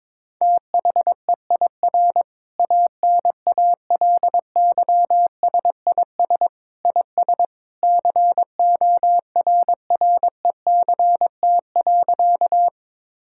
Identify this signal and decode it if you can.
Morse code, heard as a keyed tone, about 22 wpm, 700 Hz. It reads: THEIR ANALYSIS IS CORRECT.
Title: Sentences from Top 1000 Words - Review Part 5 - 22wpm